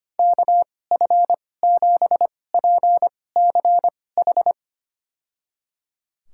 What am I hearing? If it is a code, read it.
Morse code, heard as a keyed tone, about 25 wpm, spaced as for 18 wpm, 700 Hz. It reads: KF7PCH
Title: USA Call Signs - 6 Characters - 2x Character Spacing - 25wpm